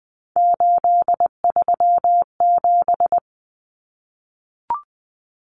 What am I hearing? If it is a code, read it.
Morse code, heard as a keyed tone, about 20 wpm, 700 Hz. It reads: 837
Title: Numbers - 3-Digit 20wpm